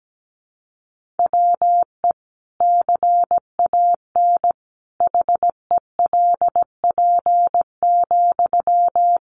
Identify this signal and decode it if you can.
Morse code, heard as a keyed tone, about 17 wpm, 700 Hz. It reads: WE CAN HELP,
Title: Sentences from Top 1500 Words 17wpm